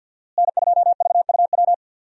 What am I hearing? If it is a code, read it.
Morse code, heard as a keyed tone, about 45 wpm, 700 Hz. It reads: N2VUW
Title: USA Call Signs - 5 Characters - Repeated 3x - 50wpm